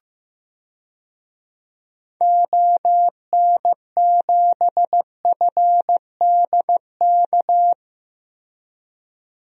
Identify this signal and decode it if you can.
Morse code, heard as a keyed tone, about 15 wpm, 700 Hz. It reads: ON7FDK
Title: DXCC-100 Call Signs - 6 Characters - Repeated 3x - 15wpm